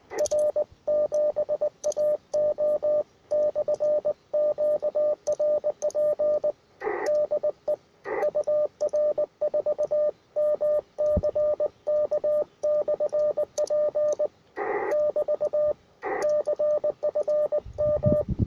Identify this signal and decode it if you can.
Morse code, heard as a keyed tone, about 20 wpm, 600 Hz, with radio noise. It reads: R7AO/QRP DE UR4MCK/P <BT> CFM
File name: attachments / R7AO UR4MCK